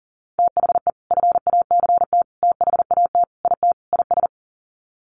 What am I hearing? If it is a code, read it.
Morse code, heard as a keyed tone, about 37 wpm, 700 Hz. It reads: THE FACT THAT IT IS
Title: Sets of 5 Words Encore2 40wpm